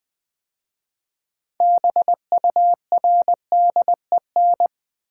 Morse code, heard as a keyed tone, about 20 wpm, 700 Hz. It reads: BURDEN
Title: 6 Letter Words - First 3 Letters Given - 20wpm